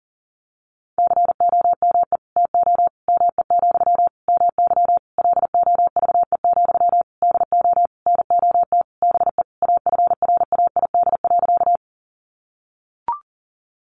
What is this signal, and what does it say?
Morse code, heard as a keyed tone, about 40 wpm, 700 Hz. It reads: COME TO ME, MY LOVE, DO NOT BE AFRAID.
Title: Sentences from Top 1500 Words 40wpm